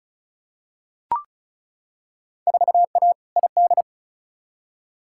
Morse code, heard as a keyed tone, about 36 wpm, 700 Hz. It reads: 4A ID
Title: Field Day Exchanges - 35wpm